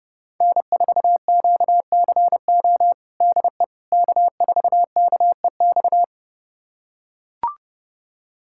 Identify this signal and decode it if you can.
Morse code, heard as a keyed tone, about 30 wpm, 700 Hz. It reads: N4QCO DE K4KEX
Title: USA Callsign DE Callsign 30wpm